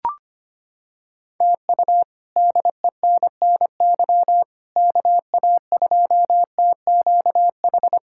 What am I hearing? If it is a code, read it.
Morse code, heard as a keyed tone, about 25 wpm, 700 Hz. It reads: TU DENNY KA2TQH
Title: K1USN SST QSO Practice - 25wpm